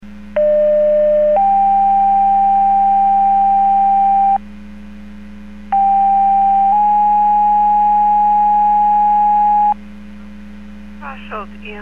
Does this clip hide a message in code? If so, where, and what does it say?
Yes: two-tone page from 0.4 to 9.7 s, heard as a one-second tone, then a three-second tone, about 600 then 790 Hz, about 790 then 830 Hz, with speech.